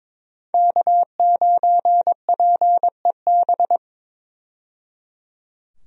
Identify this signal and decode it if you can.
Morse code, heard as a keyed tone, about 22 wpm, 700 Hz. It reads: K9PEB
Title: USA Call Signs - 5 Characters - 22wpm